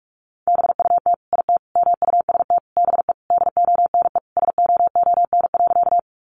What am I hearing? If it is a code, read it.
Morse code, heard as a keyed tone, about 44 wpm, 700 Hz. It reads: BUT IT MUST BE DONE SOON.
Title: Sentences from Top 500 Words 45wpm